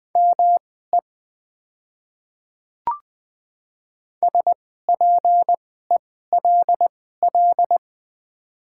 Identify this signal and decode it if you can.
Morse code, heard as a keyed tone, about 20 wpm, spaced as for 14 wpm, 700 Hz. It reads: ME SPELL
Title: 2x Character Spacing - Top 500 Words - 20wpm